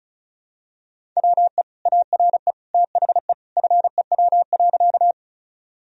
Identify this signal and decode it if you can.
Morse code, heard as a keyed tone, about 34 wpm, 700 Hz. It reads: WE ARE THE FEW.
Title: Sentences from Top 1000 Words - Review Part 1 - 35wpm